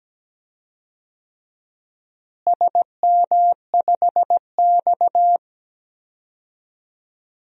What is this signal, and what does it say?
Morse code, heard as a keyed tone, about 17 wpm, 700 Hz. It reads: SM5X
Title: DXCC-100 Call Signs - 4 Characters - Repeated 2x - 17wpm